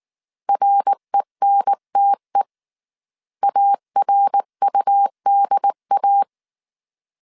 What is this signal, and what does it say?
Morse code, heard as a keyed tone, about 19 wpm, 800 Hz. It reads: RENTE ARUDA